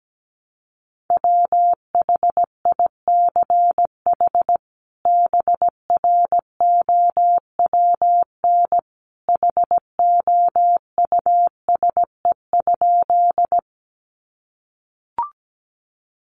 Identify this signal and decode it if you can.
Morse code, heard as a keyed tone, about 17 wpm, 700 Hz. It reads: WHICH BROWN HOUSE?
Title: Sentences from Top 1200 Words 17wpm